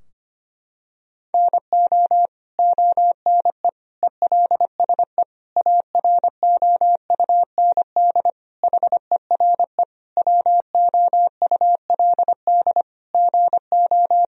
Morse code, heard as a keyed tone, about 25 wpm, 700 Hz. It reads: NO ONE ELSE AROUND HERE WOULD GO
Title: Sentences from Top 1000 Words - Review Part 3 - 25wpm